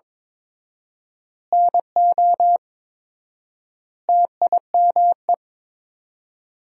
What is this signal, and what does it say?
Morse code, heard as a keyed tone, about 22 wpm, 700 Hz. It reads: NO TIME
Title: 4x Word-Spacing - Sentences from Top 400 Words - 22wpm